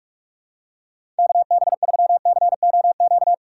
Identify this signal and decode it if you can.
Morse code, heard as a keyed tone, about 46 wpm, 700 Hz. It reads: KB3COQ
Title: USA Call Signs - 6 Characters - 4x Character Spacing - 45wpm